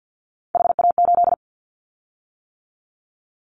Morse code, heard as a keyed tone, about 50 wpm, 700 Hz. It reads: HA8
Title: Callsign Prefixes for Top 25 DXCC - 50wpm